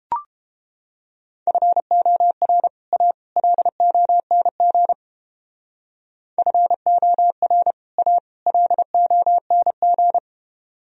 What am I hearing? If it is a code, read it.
Morse code, heard as a keyed tone, about 31 wpm, 700 Hz. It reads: FOR A LONG FOR A LONG